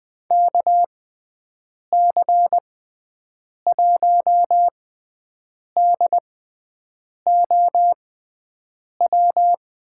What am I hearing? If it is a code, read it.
Morse code, heard as a keyed tone, about 19 wpm, spaced as for 7 wpm, 700 Hz. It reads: KC1DOW
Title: USA Call Signs - 6 Characters - 6x Character Spacing - 20wpm